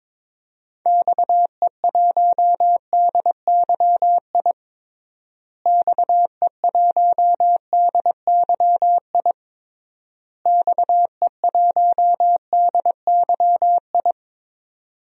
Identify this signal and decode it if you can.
Morse code, heard as a keyed tone, about 22 wpm, 700 Hz. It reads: XE1DYI XE1DYI XE1DYI
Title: DXCC-100 Call Signs - 6 Characters - Repeated 3x - 22wpm